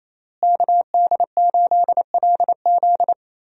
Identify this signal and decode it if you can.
Morse code, heard as a keyed tone, about 28 wpm, 700 Hz. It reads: KD8LZ
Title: USA Call Signs - 5 Characters - Repeated 3x - 28wpm